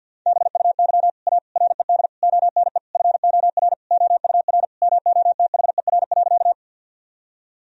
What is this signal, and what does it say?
Morse code, heard as a keyed tone, about 51 wpm, 700 Hz. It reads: BUY A RED ONE FOR OUR MOTHER.
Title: Sentences from Top 1000 Words - Review Part 2 - 50wpm